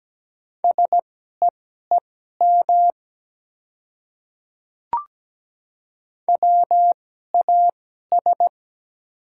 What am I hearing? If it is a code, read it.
Morse code, heard as a keyed tone, about 17 wpm, spaced as for 12 wpm, 700 Hz. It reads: SEEM WAS
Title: 2x Character Spacing - Top 300 Words - 17wpm